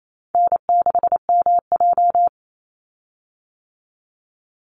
Morse code, heard as a keyed tone, about 28 wpm, 700 Hz. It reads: N6MJ
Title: Easy Call Signs - 28wpm